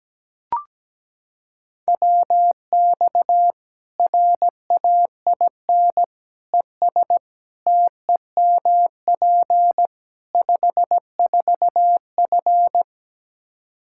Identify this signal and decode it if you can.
Morse code, heard as a keyed tone, about 17 wpm, 700 Hz. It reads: WX RAIN ES TEMP 54F
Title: Practice QSO - Exchange 2 - Info - 17wpm